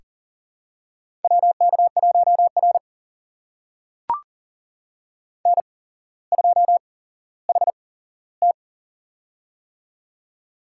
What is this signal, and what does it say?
Morse code, heard as a keyed tone, about 40 wpm, spaced as for 17 wpm, 700 Hz. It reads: WK1R N2HT
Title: USA Call Signs - 4 Characters - 8x Character Spacing - 40wpm